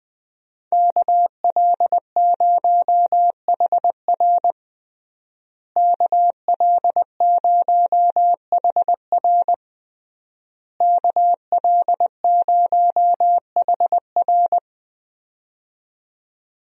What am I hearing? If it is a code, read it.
Morse code, heard as a keyed tone, about 21 wpm, 700 Hz. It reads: KL0HR KL0HR KL0HR
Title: USA Call Signs - 5 Characters - Repeated 3x - 20wpm